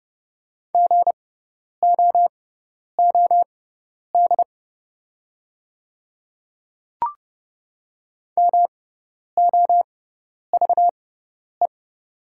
Morse code, heard as a keyed tone, about 30 wpm, spaced as for 10 wpm, 690 Hz. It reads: GOOD MOVE
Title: 6x Character Spacing - Top 300 Words - 30wpm